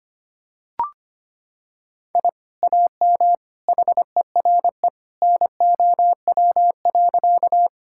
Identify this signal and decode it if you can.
Morse code, heard as a keyed tone, about 25 wpm, 700 Hz. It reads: I AM HERE NOW.
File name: practice / 25wpm / Sentences with 2-Word Phrases - 25wpm